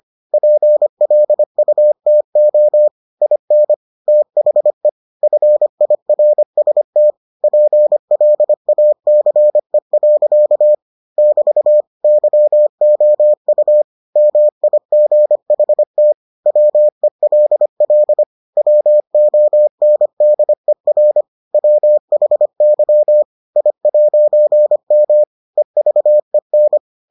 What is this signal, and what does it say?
Morse code, heard as a keyed tone, about 25 wpm, 600 Hz. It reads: PLUTO IN THE FIRST PLACE. <BT> YOU MIGHT WELL WONDER WHY I'M EVEN